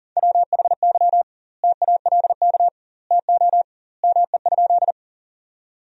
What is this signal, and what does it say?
Morse code, heard as a keyed tone, about 40 wpm, 700 Hz. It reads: WHY TALK TO ME?
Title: 2x Word-Spacing - Sentences from Top 400 Words - 40wpm